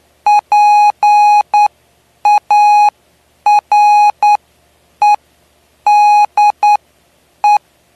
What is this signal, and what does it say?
Morse code, heard as a keyed tone, about 9 wpm, spaced as for 8 wpm, 830 Hz. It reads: PAREDE
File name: cw / py2lcd / 09-PY2LCD-Li��o 9